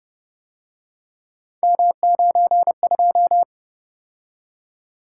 Morse code, heard as a keyed tone, about 29 wpm, 700 Hz. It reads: M92